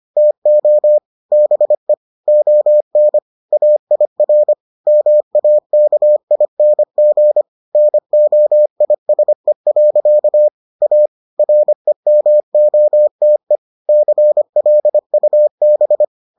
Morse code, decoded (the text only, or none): TO BE ON AIR MAKING NOISE. A REMOTE CLUB